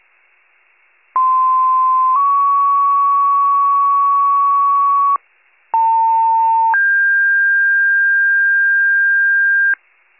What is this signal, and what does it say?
Two-tone page, heard as a one-second tone, then a three-second tone, about 1040 then 1120 Hz, about 900 then 1600 Hz. It is heard from 1.2 to 9.8 s.